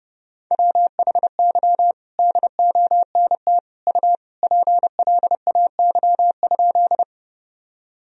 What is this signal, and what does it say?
Morse code, heard as a keyed tone, about 30 wpm, 700 Hz. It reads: WHY DONT U PLAY?
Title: Sentences from Top 300 Words and Abbreviations 30wpm